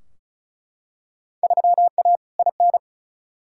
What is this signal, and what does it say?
Morse code, heard as a keyed tone, about 36 wpm, 700 Hz. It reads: 3A IN
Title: Field Day Exchanges Encore - 35wpm